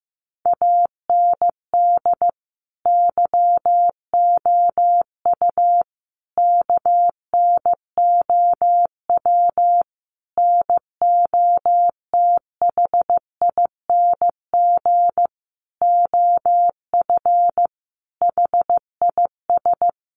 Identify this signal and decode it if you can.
Morse code, heard as a keyed tone, about 15 wpm, 700 Hz. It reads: AND YOU KNOW NOTHING OF HIS
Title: Sentences from Top 1200 Words 15wpm